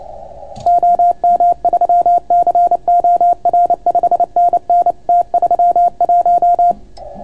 Morse code, heard as a keyed tone, about 28 wpm, 740 Hz. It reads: OM3COR 5NN T31